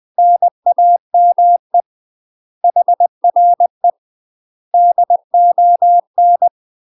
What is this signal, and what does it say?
Morse code, heard as a keyed tone, about 20 wpm, 700 Hz. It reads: NAME HRE DON
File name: UK MASTER QSOs TRAINING / 20WPM / UK MASTER QSOs 0026